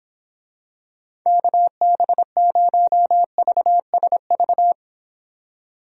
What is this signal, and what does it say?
Morse code, heard as a keyed tone, about 26 wpm, 700 Hz. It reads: KB0VSV